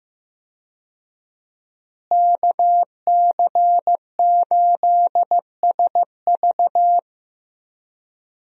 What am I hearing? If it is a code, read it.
Morse code, heard as a keyed tone, about 16 wpm, 700 Hz. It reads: KC8SV
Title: USA Call Signs - 5 Characters - Repeated 3x - 15wpm